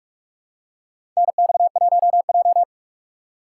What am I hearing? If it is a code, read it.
Morse code, heard as a keyed tone, about 44 wpm, 700 Hz. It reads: NX1J
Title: USA Call Signs - 4 Characters - 6x Character Spacing - 45wpm